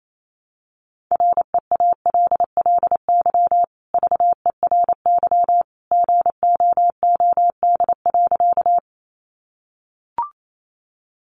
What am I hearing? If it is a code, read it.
Morse code, heard as a keyed tone, about 28 wpm, 700 Hz. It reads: REALLY VERY GOOD.
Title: Sentences from Top 600 Words 28wpm